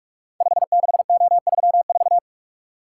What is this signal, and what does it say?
Morse code, heard as a keyed tone, about 41 wpm, 700 Hz. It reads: 56O34
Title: Code Groups - 5 Letter:Numbers - 45wpm